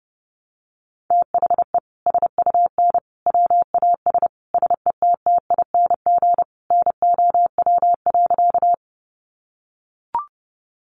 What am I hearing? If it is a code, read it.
Morse code, heard as a keyed tone, about 30 wpm, 700 Hz. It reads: THE SUN WAS SETTING NOW.
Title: Sentences from Top 1300 Words 30wpm